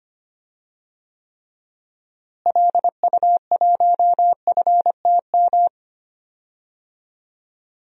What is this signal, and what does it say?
Morse code, heard as a keyed tone, about 25 wpm, 700 Hz. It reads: LU1FTM
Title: DXCC-100 Call Signs - 6 Characters - Repeated 3x - 25wpm